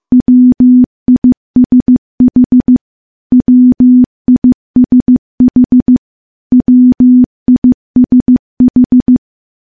Morse code, heard as a keyed tone, about 15 wpm, 250 Hz. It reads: WISH WISH WISH
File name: morse / morse-training-15WPM